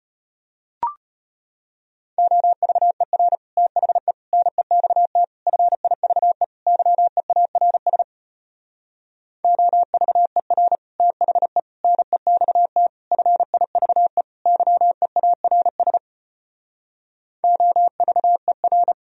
Morse code, heard as a keyed tone, about 33 wpm, 700 Hz. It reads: OVER THE NEXT FIVE YEARS OVER THE NEXT FIVE YEARS OVER